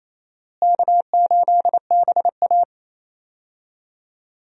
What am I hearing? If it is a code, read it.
Morse code, heard as a keyed tone, about 28 wpm, 700 Hz. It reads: K8BA
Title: USA Call Signs - 4 Characters - Repeated 3x - 28wpm